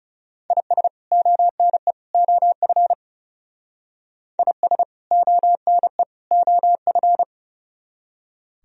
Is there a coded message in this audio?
Yes: Morse code from 0.5 to 7.2 s, heard as a keyed tone, about 38 wpm, 700 Hz. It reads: IS ONE OF IS ONE OF